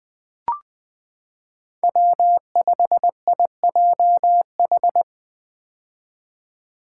Morse code, heard as a keyed tone, about 21 wpm, 700 Hz. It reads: W5IJH
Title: K1USN SST QSO Practice - 20wpm